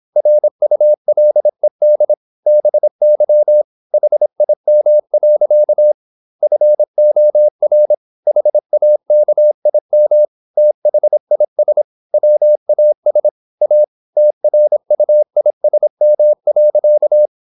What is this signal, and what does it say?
Morse code, heard as a keyed tone, about 26 wpm, 600 Hz. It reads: RULED BY HIM. FOR HAKIM THIS WAS A TRUISM.
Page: Never 26wpm | CW med Gnister